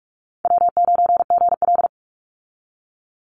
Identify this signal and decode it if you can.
Morse code, heard as a keyed tone, about 44 wpm, 700 Hz. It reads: W9GL
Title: Easy Call Signs Repeated 2x - 45wpm